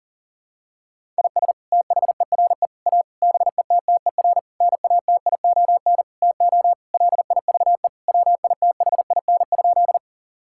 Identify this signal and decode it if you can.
Morse code, heard as a keyed tone, about 40 wpm, 700 Hz. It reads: IS THERE A BETTER NATION TO LIVE WITHIN?
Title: Sentences from Top 1000 Words - Review Part 6 - 40wpm